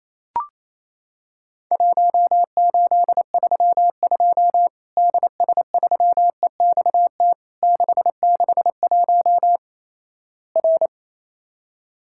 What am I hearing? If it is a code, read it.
Morse code, heard as a keyed tone, about 28 wpm, 700 Hz. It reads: 1832 DS3EXT 661 R
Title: Worked All Europe DX Contest - QTC Exchange - 28wpm